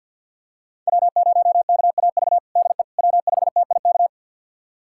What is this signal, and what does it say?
Morse code, heard as a keyed tone, about 53 wpm, 700 Hz. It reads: W0KAV DE W5TIK
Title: USA Callsign DE Callsign - 5 Characters - 50wpm